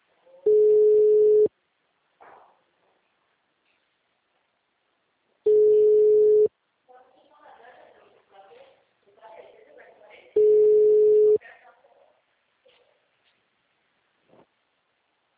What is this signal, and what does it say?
Ringback tone (Europe, 425 Hz), with speech.
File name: out- -2024- - - .501